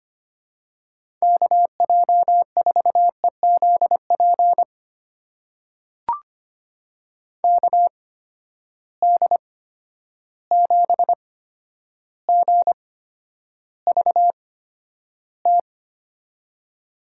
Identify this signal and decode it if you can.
Morse code, heard as a keyed tone, about 25 wpm, spaced as for 11 wpm, 700 Hz. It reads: KJ4EZP KD7GVT
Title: USA Call Signs - 6 Characters - 8x Character Spacing - 25wpm